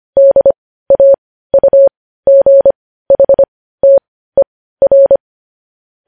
Morse code, heard as a keyed tone, about 25 wpm, spaced as for 15 wpm, 550 Hz. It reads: DAUGHTER
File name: podcast / News headlines-1x-25x15